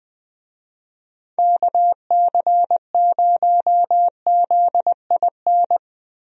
Morse code, heard as a keyed tone, about 21 wpm, 700 Hz. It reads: KC0ZIN
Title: USA Call Signs - 6 Characters - Repeated 3x - 20wpm